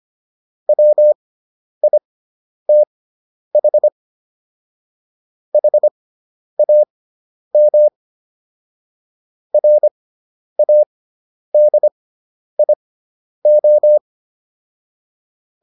Morse code, heard as a keyed tone, about 25 wpm, spaced as for 10 wpm, 600 Hz. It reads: WITH HAM RADIO